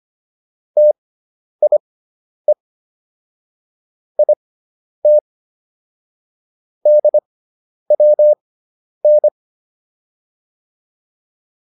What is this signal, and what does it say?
Morse code, heard as a keyed tone, about 25 wpm, spaced as for 10 wpm, 600 Hz. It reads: TIE IT DWN